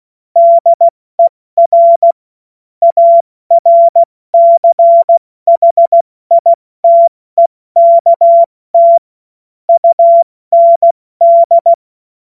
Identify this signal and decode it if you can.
Morse code, heard as a keyed tone, about 16 wpm, 680 Hz. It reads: DER ARCHITEKT UND
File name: Morsekurs / Witze / Witz 20. Januar 2026 70 F80